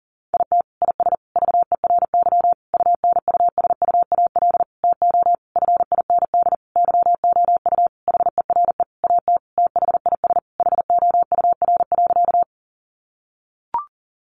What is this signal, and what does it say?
Morse code, heard as a keyed tone, about 40 wpm, 700 Hz. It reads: IT IS VERY UNUSUAL TO FIND YOU HERE AT THIS HOUR.
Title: Sentences from Top 1400 Words 40wpm